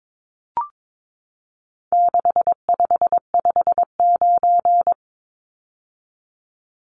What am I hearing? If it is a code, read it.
Morse code, heard as a keyed tone, about 22 wpm, 700 Hz. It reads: 6559
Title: Numbers - 4-Digit 22wpm